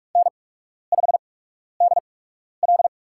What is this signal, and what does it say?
Morse code, heard as a keyed tone, about 43 wpm, spaced as for 12 wpm, 700 Hz. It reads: N5DL
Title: USA Call Signs - 4 Characters - 8x Character Spacing - 45wpm